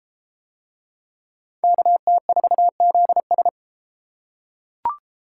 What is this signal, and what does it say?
Morse code, heard as a keyed tone, about 31 wpm, 700 Hz. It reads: KT4ZS